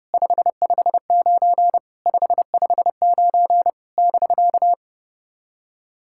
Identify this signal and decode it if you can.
Morse code, heard as a keyed tone, about 30 wpm, 700 Hz. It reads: 559 559 <BK>
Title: SOTA QSO Practice 30wpm